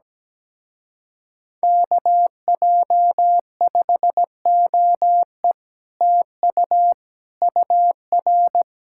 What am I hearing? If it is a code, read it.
Morse code, heard as a keyed tone, about 17 wpm, 700 Hz. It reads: KJ5OE TU UR
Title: Speed-Racing Common QSO Phrases Encore - 17wpm